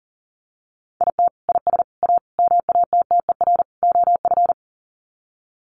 Morse code, heard as a keyed tone, about 40 wpm, 700 Hz. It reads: IT IS A MATTER OF
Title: Sets of 5 Words Encore2 40wpm